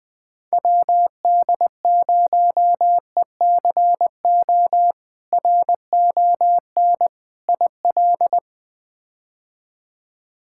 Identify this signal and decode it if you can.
Morse code, heard as a keyed tone, about 20 wpm, 700 Hz. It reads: WD0ECO RON IL
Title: K1USN SST QSO Practice - 20wpm